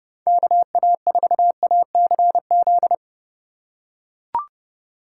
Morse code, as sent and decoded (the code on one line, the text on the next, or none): -.- .- ....- .- -.-. --..
KA4ACZ